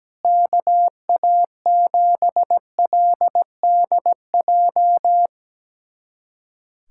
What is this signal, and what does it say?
Morse code, heard as a keyed tone, about 17 wpm, 700 Hz. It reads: KA7LDJ
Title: USA Callsign DE Callsign - 6 Characters - 17wpm